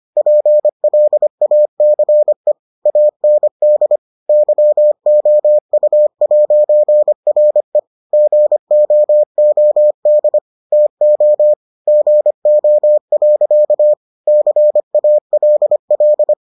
Morse code, heard as a keyed tone, about 25 wpm, 600 Hz. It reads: PLACE AND YOU'RE GOOD TO GO. CALL